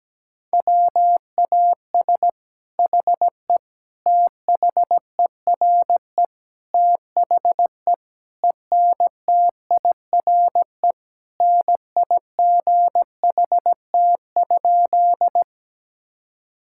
Morse code, heard as a keyed tone, about 17 wpm, 700 Hz. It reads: WAS HE THERE THE ENTIRE NIGHT?
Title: Sentences from Top 1000 Words - Review Part 5 - 17wpm